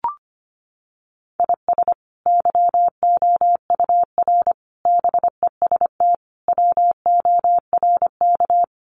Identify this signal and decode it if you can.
Morse code, heard as a keyed tone, about 25 wpm, 700 Hz. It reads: IS YOUR BEST WORK
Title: Sentences from Top 1200 Words 25wpm